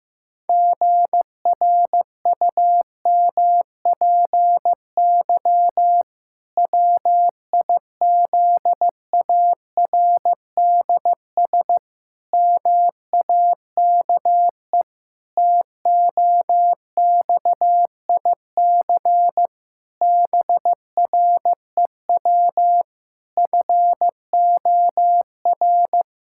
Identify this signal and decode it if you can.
Morse code, heard as a keyed tone, about 15 wpm, 700 Hz. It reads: GRUMPY WIZARDS MAKE TOXIC BREW FOR
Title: Pangram Challenge v2 - 15wpm